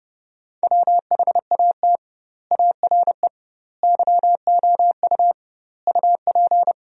Morse code, heard as a keyed tone, about 32 wpm, 700 Hz. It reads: WHAT ARE YOU UP
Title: 2x Word-Spacing - Sentences from Top 100 Words - 30wpm